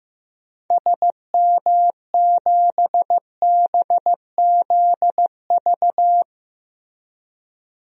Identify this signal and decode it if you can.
Morse code, heard as a keyed tone, about 15 wpm, 700 Hz. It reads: SM7BZV
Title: DXCC-100 Call Signs - 6 Characters - Repeated 3x - 15wpm